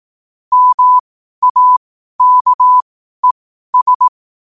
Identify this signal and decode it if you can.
Morse code, heard as a keyed tone, about 18 wpm, spaced as for 12 wpm, 1000 Hz. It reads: MAKES
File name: qotdfiles / 12wpm